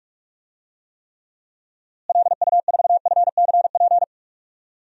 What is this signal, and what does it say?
Morse code, heard as a keyed tone, about 46 wpm, 700 Hz. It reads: LU4FCP
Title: DXCC-100 Call Signs - 6 Characters - Repeated 2x - 45wpm